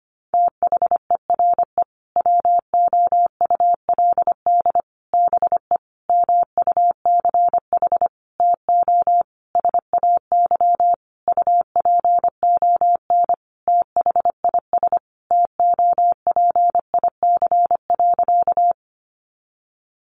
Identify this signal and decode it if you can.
Morse code, heard as a keyed tone, about 25 wpm, 700 Hz. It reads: THERE WOULD BE MUCH TO SAY UPON THIS TOPIC.
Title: Sentences from Top 1200 Words 25wpm